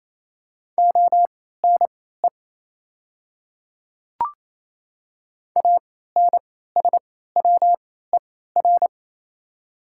Morse code, heard as a keyed tone, about 27 wpm, spaced as for 16 wpm, 690 Hz. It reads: ONE ANSWER
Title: 3x Character Spacing - Top 300 Words - 28wpm